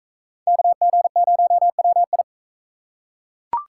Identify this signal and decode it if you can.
Morse code, heard as a keyed tone, about 40 wpm, 700 Hz. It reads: KG0WI